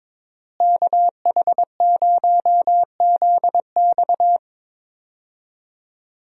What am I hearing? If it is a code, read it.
Morse code, heard as a keyed tone, about 22 wpm, 700 Hz. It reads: KH0ZX
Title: USA Call Signs - 5 Characters - Repeated 3x - 22wpm